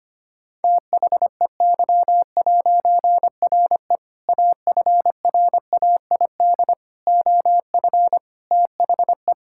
Morse code, heard as a keyed tone, about 25 wpm, 700 Hz. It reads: THEY'RE AFRAID OF THE
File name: practice / 25wpm / All-Age Appropriate Jokes - 25wpm